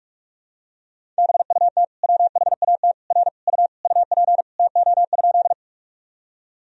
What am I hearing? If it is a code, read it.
Morse code, heard as a keyed tone, about 41 wpm, 700 Hz. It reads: BUT WHAT R U UP TO?
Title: Sentences from Top 100 Words and Abbreviations 45wpm